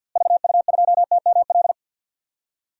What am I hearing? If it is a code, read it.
Morse code, heard as a keyed tone, about 46 wpm, 700 Hz. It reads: VU2TML
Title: DXCC-100 Call Signs - 6 Characters - Repeated 2x - 50wpm